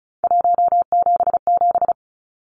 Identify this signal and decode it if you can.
Morse code, heard as a keyed tone, about 34 wpm, 700 Hz. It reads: 177
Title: Numbers - 3-Digit 35wpm